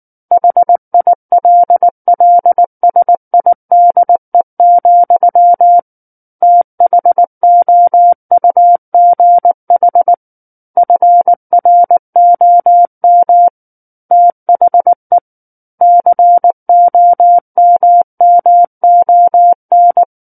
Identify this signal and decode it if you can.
Morse code, heard as a keyed tone, about 19 wpm, 700 Hz. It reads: HILLSIDE, THOUGH FROM THE COMMON
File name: Book One / 10-Chapter 10 - 19 WPM